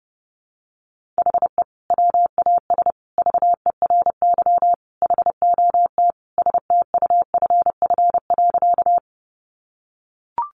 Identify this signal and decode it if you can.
Morse code, heard as a keyed tone, about 30 wpm, 700 Hz. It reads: HE WAS VERY HOT STUFF.
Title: Sentences from Top 1400 Words 30wpm